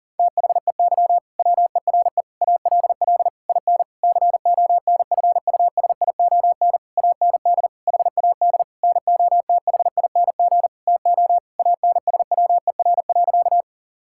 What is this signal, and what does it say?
Morse code, heard as a keyed tone, about 40 wpm, 700 Hz. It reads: THEY WERE ALL IN CONFUSION AND HAD NOTHING TO ANSWER.
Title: Sentences from Top 1600 Words 40wpm